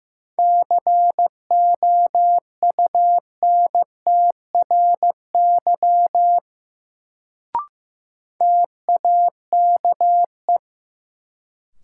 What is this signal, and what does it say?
Morse code, heard as a keyed tone, about 15 wpm, 700 Hz. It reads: COUNTRY TAKE
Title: Sets of 2 Words - New Word Drill - 15wpm